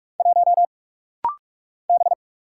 Morse code, heard as a keyed tone, about 46 wpm, 700 Hz. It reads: 1 B
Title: Single Letter-Number - Rapid-Fire 45wpm